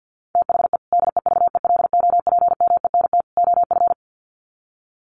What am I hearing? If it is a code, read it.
Morse code, heard as a keyed tone, about 50 wpm, 700 Hz. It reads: THE DEVELOPMENT OF
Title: Sets of 3 Words 50wpm